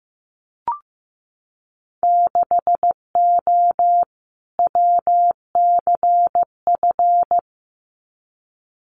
Morse code, heard as a keyed tone, about 15 wpm, 700 Hz. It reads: 6O WCF
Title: Winter Field Day Exchanges Encore 15wpm